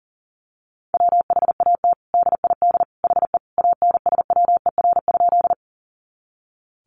Morse code, heard as a keyed tone, about 40 wpm, 700 Hz. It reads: WHAT DID HE ANSWER?
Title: Sentences from Top 300 Words 40wpm